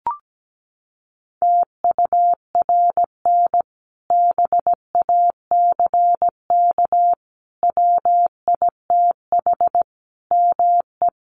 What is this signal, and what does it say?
Morse code, heard as a keyed tone, about 17 wpm, 700 Hz. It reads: TURN BACK WITH ME
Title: Sentences from Top 1300 Words 17wpm